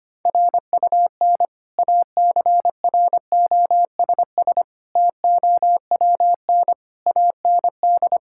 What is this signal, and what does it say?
Morse code, heard as a keyed tone, about 25 wpm, 700 Hz. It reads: RUN ACROSS TOWN AND
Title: Sentences from Top 1000 Words - Review Part 5 - 25wpm